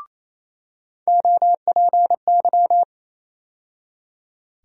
Morse code, heard as a keyed tone, about 28 wpm, 700 Hz. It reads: OPY
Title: USA Call Sign Postfixes - 3 Letters 28wpm